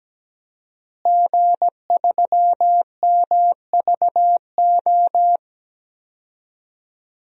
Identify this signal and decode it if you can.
Morse code, heard as a keyed tone, about 17 wpm, 700 Hz. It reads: G3MVO
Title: DXCC-100 Call Signs - 5 Characters - Repeated 2x - 17wpm